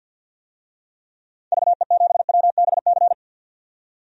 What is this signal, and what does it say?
Morse code, heard as a keyed tone, about 50 wpm, 700 Hz. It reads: VE7WBC
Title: DXCC-100 Call Signs - 6 Characters - 50wpm